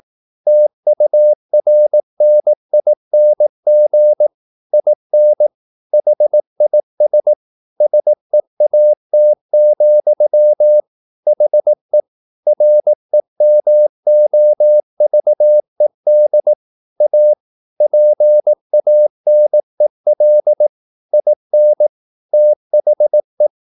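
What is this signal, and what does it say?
Morse code, heard as a keyed tone, about 18 wpm, 600 Hz. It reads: TURNING IN HIS SEAT, HE REMOVED A PANEL IN THE
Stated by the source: Morse practice at 18 wpm, English text, from CW med Gnister.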